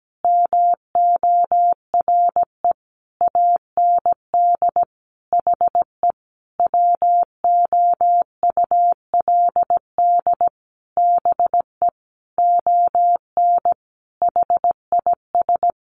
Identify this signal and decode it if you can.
Morse code, heard as a keyed tone, about 17 wpm, 700 Hz. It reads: MORE AND HE WOULD BE ON HIS
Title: Sentences from Top 300 Words 17wpm